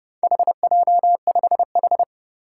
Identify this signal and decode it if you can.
Morse code, heard as a keyed tone, about 30 wpm, 700 Hz. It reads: HJ5H
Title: Four Challenging Letter-Number Combinations 30wpm